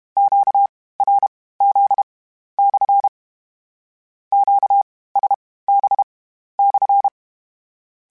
Morse code, heard as a keyed tone, about 32 wpm, spaced as for 16 wpm, 800 Hz. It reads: QRZ/ QSB/